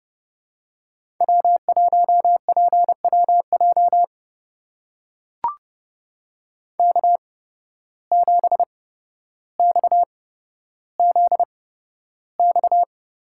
Morse code, heard as a keyed tone, about 30 wpm, spaced as for 13 wpm, 700 Hz. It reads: W1PWJ K7XZX
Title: USA Call Signs - 5 Characters - 8x Character Spacing - 30wpm